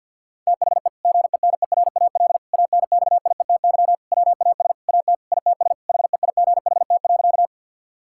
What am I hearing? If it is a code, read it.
Morse code, heard as a keyed tone, about 50 wpm, 700 Hz. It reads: THE GENERAL ANXIETY WAS AT ITS HEIGHT.
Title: Sentences from Top 1600 Words 50wpm